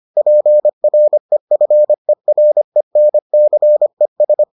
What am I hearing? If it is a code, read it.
Morse code, heard as a keyed tone, about 25 wpm, 600 Hz. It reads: PREFERENCES